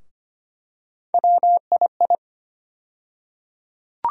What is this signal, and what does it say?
Morse code, heard as a keyed tone, about 25 wpm, 700 Hz. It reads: WII
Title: USA Call Sign Postfixes - 3 Letters 25wpm